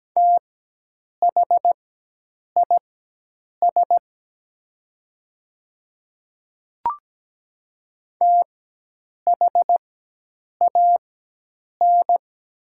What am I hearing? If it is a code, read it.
Morse code, heard as a keyed tone, about 17 wpm, spaced as for 8 wpm, 700 Hz. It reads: THIS THAN
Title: 4x Character Spacing - Top 100 Words - 17wpm